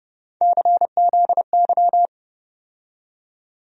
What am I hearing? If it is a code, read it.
Morse code, heard as a keyed tone, about 30 wpm, 700 Hz. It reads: CZY
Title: USA Call Sign Postfixes - 3 Letters 30wpm